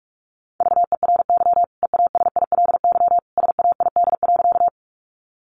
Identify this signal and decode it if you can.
Morse code, heard as a keyed tone, about 44 wpm, 700 Hz. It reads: VERY EASILY SAID.
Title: Sentences from Top 900 Words 45wpm